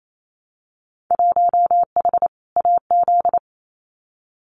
Morse code, heard as a keyed tone, about 28 wpm, 700 Hz. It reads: 1H AZ
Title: Winter Field Day Exchanges 28wpm